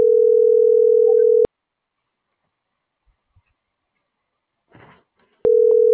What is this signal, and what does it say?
Ringback tone (North America), 440 + 480 Hz.